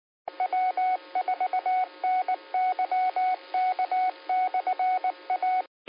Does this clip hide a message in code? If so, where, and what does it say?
Yes: Morse code from 0.4 to 5.6 s, heard as a keyed tone, about 19 wpm, 740 Hz, with radio noise. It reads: W4NYK/A